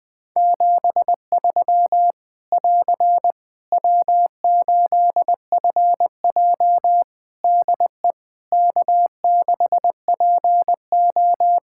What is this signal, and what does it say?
Morse code, heard as a keyed tone, about 20 wpm, 700 Hz. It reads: 73 <AR> W8FJ DE K6PO
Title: Practice QSO - Exchange 4 - Ending - 20wpm